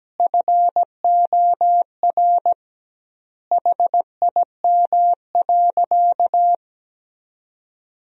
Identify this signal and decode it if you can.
Morse code, heard as a keyed tone, about 17 wpm, 700 Hz. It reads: FOR HIM.
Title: 2x Word-Spacing - Sentences from Top 500 Words - 17wpm